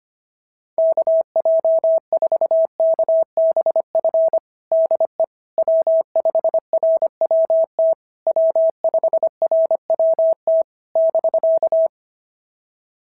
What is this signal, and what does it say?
Morse code, heard as a keyed tone, about 25 wpm, 690 Hz. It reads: KJ4KBF DE W5RWT W5RWT <BK>
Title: SOTA QSO Practice 25wpm